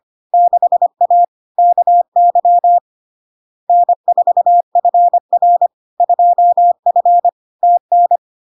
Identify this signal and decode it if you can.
Morse code, heard as a keyed tone, about 25 wpm, 700 Hz. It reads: 6A KY N4FR 2F TN
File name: LARK-FD-2022-Log-in-CW-25wpm-700-Hz